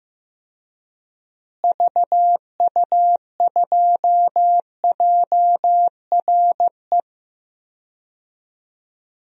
Morse code, heard as a keyed tone, about 16 wpm, 700 Hz. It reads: VU2JRE
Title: DXCC-100 Call Signs - 6 Characters - Repeated 3x - 15wpm